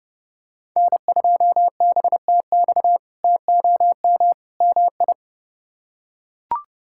Morse code, heard as a keyed tone, about 32 wpm, 700 Hz. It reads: N2BTX TOM MI
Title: K1USN SST QSO Practice - 30wpm